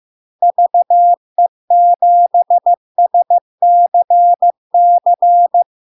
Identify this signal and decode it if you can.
Morse code, heard as a keyed tone, about 15 wpm, 700 Hz. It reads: VE7SCC
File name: LARK-FD-2022-Log-in-CW-15wpm-700-Hz